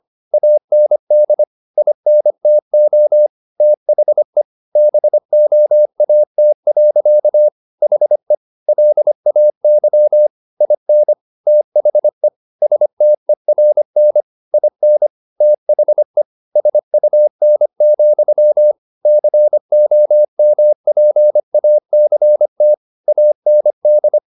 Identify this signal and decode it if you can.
Morse code, heard as a keyed tone, about 25 wpm, 600 Hz. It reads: AND INTO THE BOAT. HE LAY IN THE STERN IN THE SUN, COMPACT AND
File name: the old man and the sea part 1 25wpm